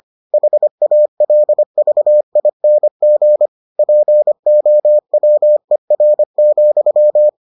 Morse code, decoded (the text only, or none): HALVING POWER,